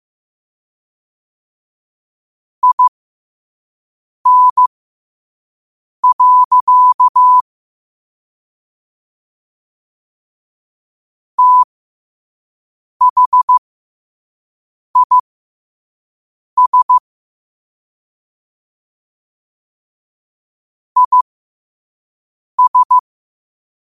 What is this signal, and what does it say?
Morse code, heard as a keyed tone, about 15 wpm, spaced as for 5 wpm, 1000 Hz. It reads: IN. THIS IS
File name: qotdfiles / 5wpm